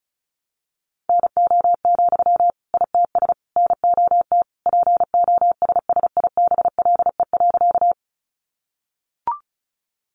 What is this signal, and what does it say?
Morse code, heard as a keyed tone, about 34 wpm, 700 Hz. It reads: NO, ITS NOT POSSIBLE.
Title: Sentences from Top 600 Words 35wpm